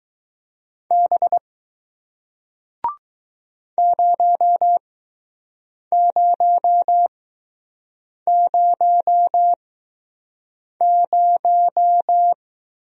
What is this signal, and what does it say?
Morse code, heard as a keyed tone, about 20 wpm, spaced as for 6 wpm, 700 Hz. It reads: B 0000